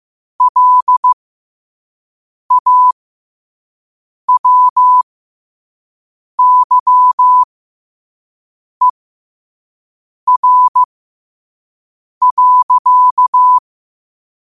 Morse code, heard as a keyed tone, about 15 wpm, spaced as for 5 wpm, 1000 Hz. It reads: LAWYER.
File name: qotdfiles / 5wpm